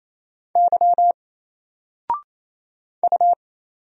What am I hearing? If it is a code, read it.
Morse code, heard as a keyed tone, about 27 wpm, 700 Hz. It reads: Y U
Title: Single Letter-Number Alt 28wpm